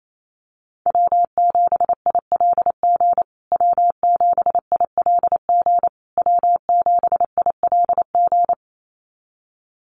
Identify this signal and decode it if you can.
Morse code, heard as a keyed tone, about 28 wpm, 700 Hz. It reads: W7ILG W7ILG W7ILG
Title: USA Call Signs Repeated 3x 28wpm